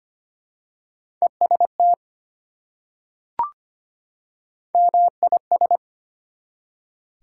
Morse code, heard as a keyed tone, about 25 wpm, 700 Hz. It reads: EST MIS
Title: Prefixes and Suffixes 25wpm